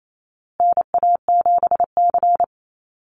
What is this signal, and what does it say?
Morse code, heard as a keyed tone, about 28 wpm, 700 Hz. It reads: NA7C
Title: Call Signs Repeated 2x - 28wpm